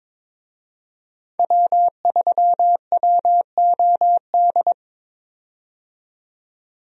Morse code, heard as a keyed tone, about 22 wpm, 700 Hz. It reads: W3WOD
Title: USA Call Signs - 5 Characters - 22wpm